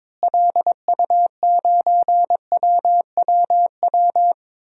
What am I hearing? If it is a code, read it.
Morse code, heard as a keyed tone, about 22 wpm, 700 Hz. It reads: LU9WWW
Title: DXCC-100 Call Signs - 6 Characters - Repeated 2x - 22wpm